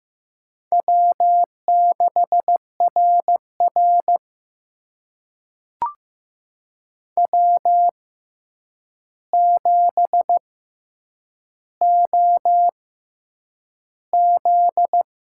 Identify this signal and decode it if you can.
Morse code, heard as a keyed tone, about 15 wpm, spaced as for 8 wpm, 700 Hz. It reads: W6RR W7OZ
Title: USA Call Signs - 4 Characters - 6x Character Spacing - 15wpm